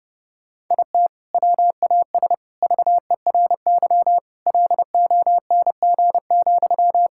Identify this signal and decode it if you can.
Morse code, heard as a keyed tone, about 32 wpm, 700 Hz. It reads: IT WAS VERY LONG,
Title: Sentences from Top 1600 Words 30wpm